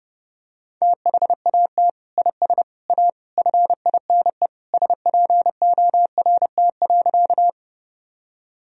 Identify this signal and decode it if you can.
Morse code, heard as a keyed tone, about 30 wpm, 700 Hz. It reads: THAT IS A FINE SPORT.
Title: Sentences from Top 1000 Words - Review Part 4 - 30wpm